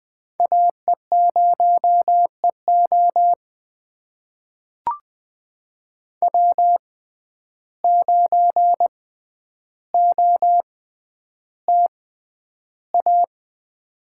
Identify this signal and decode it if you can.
Morse code, heard as a keyed tone, about 20 wpm, spaced as for 10 wpm, 700 Hz. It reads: AE0EO W9OTA
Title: USA Call Signs - 5 Characters - 6x Character Spacing - 20wpm